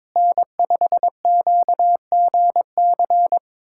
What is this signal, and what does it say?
Morse code, heard as a keyed tone, about 22 wpm, 700 Hz. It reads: N5QGC
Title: USA Callsign DE Callsign 22wpm